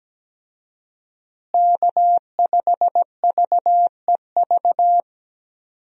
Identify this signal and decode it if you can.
Morse code, heard as a keyed tone, about 17 wpm, 700 Hz. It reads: K5VEV
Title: USA Call Signs - 5 Characters - Repeated 3x - 17wpm